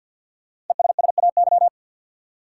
Morse code, heard as a keyed tone, about 53 wpm, 700 Hz. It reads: ESSAY
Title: Sentences from Top 1500 Words - New Word Drill Alt - 50wpm